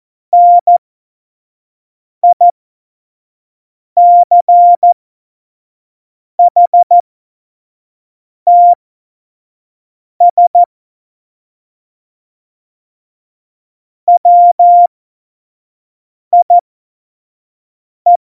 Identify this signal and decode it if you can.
Morse code, heard as a keyed tone, about 14 wpm, spaced as for 5 wpm, 700 Hz. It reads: NICHTS WIE
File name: Morsekurs / Witze / Witz 12. Februar 2019, 25 F70